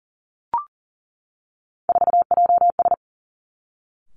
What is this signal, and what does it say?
Morse code, heard as a keyed tone, about 39 wpm, 700 Hz. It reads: 4JS
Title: Three Challenging Letter-Number Combinations 40wpm